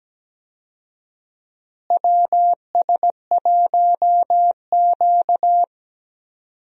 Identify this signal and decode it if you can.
Morse code, heard as a keyed tone, about 17 wpm, 700 Hz. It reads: WS1Q
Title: USA Call Signs - 4 Characters - Repeated 3x - 17wpm